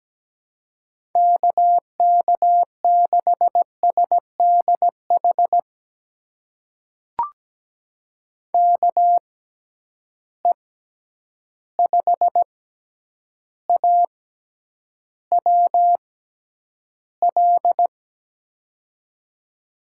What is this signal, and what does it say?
Morse code, heard as a keyed tone, about 17 wpm, spaced as for 9 wpm, 700 Hz. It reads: KK6SDH KE5AWL
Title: USA Call Signs - 6 Characters - 6x Character Spacing - 17wpm